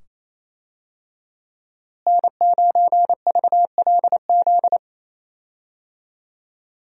Morse code, heard as a keyed tone, about 28 wpm, 700 Hz. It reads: N9VLZ
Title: USA Call Signs - 5 Characters - Repeated 2x - 28wpm